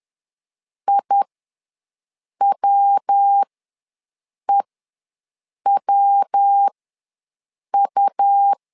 Morse code, heard as a keyed tone, about 10 wpm, spaced as for 6 wpm, 800 Hz. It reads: IWEWU
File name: LEZIONI / 3 MWRO / 3 MW ws 10wpm 2213